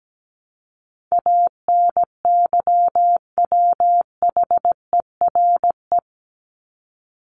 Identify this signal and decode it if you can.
Morse code, heard as a keyed tone, about 17 wpm, 700 Hz. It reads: ANYWHERE
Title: Top 1251-1500 Words 17wpm